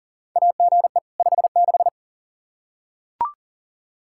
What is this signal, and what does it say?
Morse code, heard as a keyed tone, about 40 wpm, 700 Hz. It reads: AGE 56
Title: K1USN SST QSO Practice 40wpm